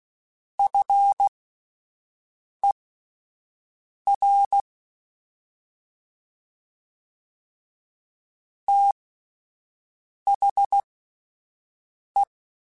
Morse code, heard as a keyed tone, about 16 wpm, spaced as for 5 wpm, 800 Hz. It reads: FER THE
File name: (spaced) / QSOs Spaced-mp3s / WA7QOX 2 @10 sp